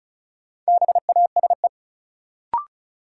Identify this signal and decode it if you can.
Morse code, heard as a keyed tone, about 36 wpm, 700 Hz. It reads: BASE
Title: Sentences from Top 1000 Words - Review Part 1 - New Word Drill - 35wpm